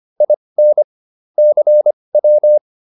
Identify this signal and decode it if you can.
Morse code, heard as a keyed tone, about 25 wpm, spaced as for 20 wpm, 600 Hz. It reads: IN CW